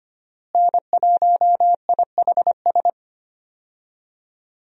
Morse code, heard as a keyed tone, about 25 wpm, 700 Hz. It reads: N1IHS
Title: USA Call Signs - 5 Characters - Repeated 3x - 25wpm